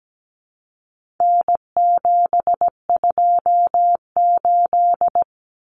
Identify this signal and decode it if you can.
Morse code, heard as a keyed tone, about 17 wpm, 700 Hz. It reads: N728
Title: Common Cut Numbers - 4-Digit 17wpm